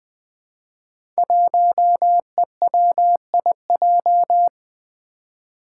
Morse code, heard as a keyed tone, about 20 wpm, 700 Hz. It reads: 1EWIJ
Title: Five Challenging Letter-Number Combinations 20wpm